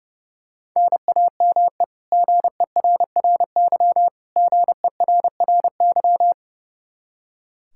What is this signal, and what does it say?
Morse code, heard as a keyed tone, about 30 wpm, 700 Hz. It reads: NAME GERRY GERRY
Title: Practice QSO - Exchange 1 - Beginning - 30wpm